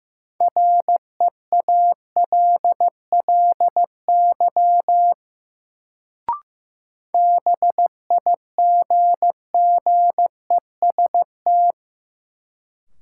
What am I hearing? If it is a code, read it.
Morse code, heard as a keyed tone, about 15 wpm, 700 Hz. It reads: REALLY BIGGEST
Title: Sets of 4 Words - New Word Drill - 15wpm